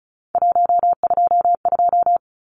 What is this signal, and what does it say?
Morse code, heard as a keyed tone, about 34 wpm, 700 Hz. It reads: 122
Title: Numbers - 3-Digit 35wpm